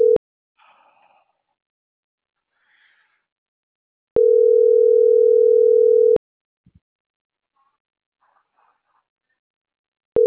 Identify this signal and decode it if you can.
Ringback tone (North America), 440 + 480 Hz.